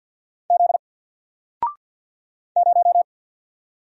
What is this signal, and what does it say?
Morse code, heard as a keyed tone, about 50 wpm, 700 Hz. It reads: Z 0
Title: Single Character (Letter:Number:Punctuation:Prosign) 50wpm